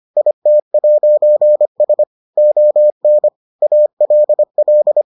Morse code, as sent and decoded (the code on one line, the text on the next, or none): .. - .----. ... / --- -. / .- .-.. .-..
IT'S ON ALL